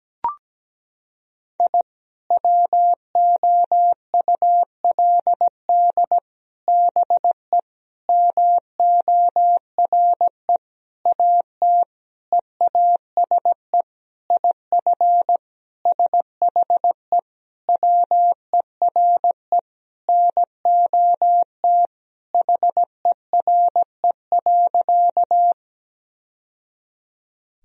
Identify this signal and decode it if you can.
Morse code, heard as a keyed tone, about 17 wpm, 700 Hz. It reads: I WOULD BE MORE AT EASE IF SHE WERE NOT HERE.
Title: Sentences from Top 1000 Words - Review Part 1 - 17wpm